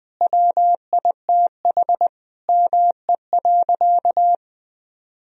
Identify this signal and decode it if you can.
Morse code, heard as a keyed tone, about 21 wpm, 700 Hz. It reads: WITH ME.
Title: Sentences from Top 1000 Words - Review Part 1 - 20wpm